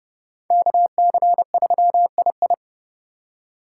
Morse code, heard as a keyed tone, about 31 wpm, 700 Hz. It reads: KC3II